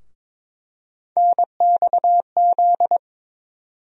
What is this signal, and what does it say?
Morse code, heard as a keyed tone, about 22 wpm, 700 Hz. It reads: NXZ
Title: USA Call Sign Postfixes - 3 Letters 22wpm